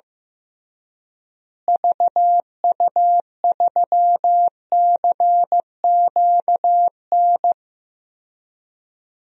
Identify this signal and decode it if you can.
Morse code, heard as a keyed tone, about 15 wpm, 700 Hz. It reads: VU3CQN
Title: DXCC-100 Call Signs - 6 Characters - Repeated 3x - 15wpm